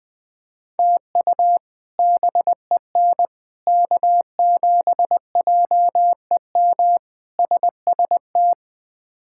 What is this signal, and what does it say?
Morse code, heard as a keyed tone, about 20 wpm, 700 Hz. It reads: TU BEN K7JEM SST
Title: K1USN SST QSO Practice - 20wpm